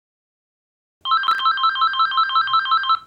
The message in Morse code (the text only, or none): none